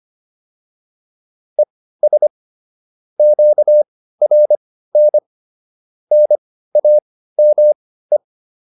Morse code, heard as a keyed tone, about 25 wpm, spaced as for 15 wpm, 600 Hz. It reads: ES QRN NAME
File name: QSO 408 15